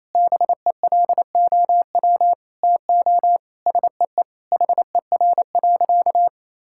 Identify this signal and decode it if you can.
Morse code, heard as a keyed tone, about 28 wpm, 700 Hz. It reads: BELOW TO SEE HER.
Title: Sentences from Top 1000 Words - Review Part 5 - 28wpm